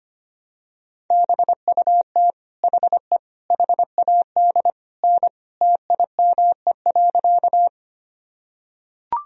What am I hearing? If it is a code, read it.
Morse code, heard as a keyed tone, about 25 wpm, 700 Hz. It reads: BUT HE HAD N TIME.
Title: Sentences from Top 100 Words and Abbreviations 25wpm